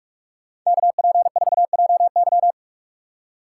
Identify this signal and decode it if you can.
Morse code, heard as a keyed tone, about 59 wpm, 700 Hz. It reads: KP4JY